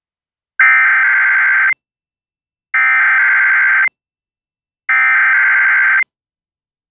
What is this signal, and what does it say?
Emergency Alert System (SAME) tones (header bursts), heard as harsh buzzing bursts. It is heard from 0.6 to 6.0 s.